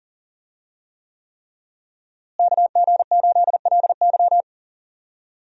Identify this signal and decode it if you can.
Morse code, heard as a keyed tone, about 40 wpm, 700 Hz. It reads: KG8LY
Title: USA Call Signs - 5 Characters - Repeated 3x - 40wpm